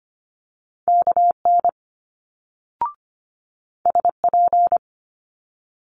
Morse code, heard as a keyed tone, about 25 wpm, 700 Hz. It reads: KN SP
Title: Multi-letter Phonemes Quick Fire 25wpm